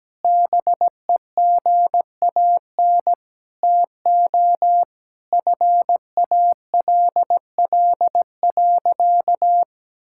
Morse code, heard as a keyed tone, about 17 wpm, 700 Hz. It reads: BEGAN TO FALL.
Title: Sentences from Top 1000 Words - Review Part 6 - 17wpm